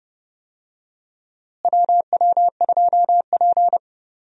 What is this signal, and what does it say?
Morse code, heard as a keyed tone, about 30 wpm, 700 Hz. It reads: WW2P
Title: USA Call Signs - 4 Characters - Repeated 2x - 30wpm